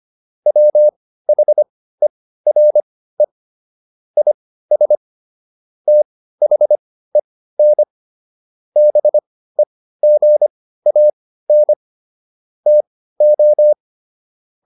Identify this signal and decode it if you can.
Morse code, heard as a keyed tone, about 25 wpm, spaced as for 15 wpm, 600 Hz. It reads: WHERE IS THEN BEGAN TO